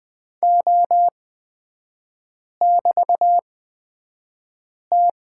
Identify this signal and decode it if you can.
Morse code, heard as a keyed tone, about 20 wpm, spaced as for 5 wpm, 700 Hz. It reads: O<BT>T
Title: 224 Lesson 36e - BT - Three Characters 20wpm